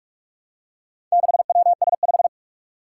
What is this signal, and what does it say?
Morse code, heard as a keyed tone, about 44 wpm, 700 Hz. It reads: 6WS5
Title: Four Challenging Letter-Number Combinations 45wpm